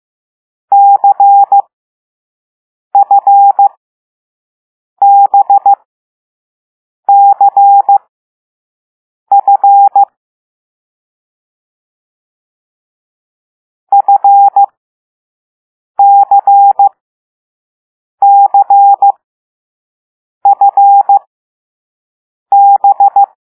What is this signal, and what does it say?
Morse code, heard as a keyed tone, about 15 wpm, spaced as for 5 wpm, 800 Hz. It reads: CFBCF FCCFB
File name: Morse / Level6B